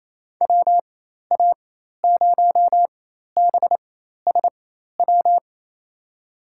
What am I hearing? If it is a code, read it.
Morse code, heard as a keyed tone, about 28 wpm, spaced as for 13 wpm, 700 Hz. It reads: WA0BSW
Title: USA Call Signs - 4x Character Spacing - 28wpm